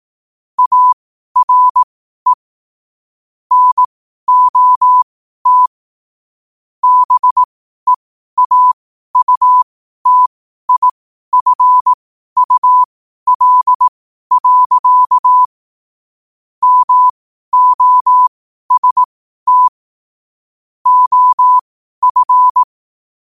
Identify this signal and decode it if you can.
Morse code, heard as a keyed tone, about 18 wpm, spaced as for 12 wpm, 1000 Hz. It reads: ARE NOT BEAUTIFUL. MOST OF
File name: qotdfiles / 12wpm